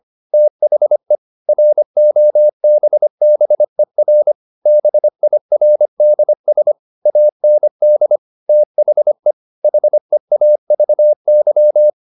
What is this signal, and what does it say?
Morse code, heard as a keyed tone, about 25 wpm, 600 Hz. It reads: THE ROBBER BIRDS AND THE HEAVY
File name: the old man and the sea part 1 25wpm